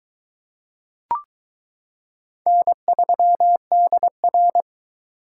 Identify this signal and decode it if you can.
Morse code, heard as a keyed tone, about 22 wpm, 700 Hz. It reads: N3DR